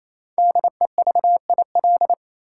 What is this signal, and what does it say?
Morse code, heard as a keyed tone, about 28 wpm, 700 Hz. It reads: DEVIL
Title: Five Letter Words - 28wpm